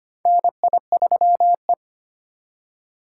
Morse code, heard as a keyed tone, about 25 wpm, 700 Hz. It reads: NI3E
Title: USA Call Signs - 4 Characters - 2x Character Spacing - 25wpm